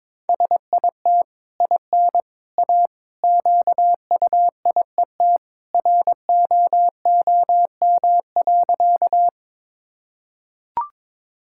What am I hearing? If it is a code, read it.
Morse code, heard as a keyed tone, about 22 wpm, 700 Hz. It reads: SIT IN A QUIET ROOM.
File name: practice / 22wpm / Easy Sentences with Hints - 22wpm